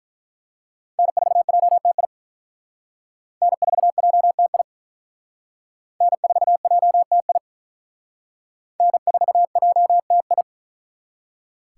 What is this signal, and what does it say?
Morse code, heard as a keyed tone, about 44 wpm, 700 Hz. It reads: N4JTI N4JTI N4JTI N4JTI
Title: Speed Racing - USA 1x3 Call Signs 35wpm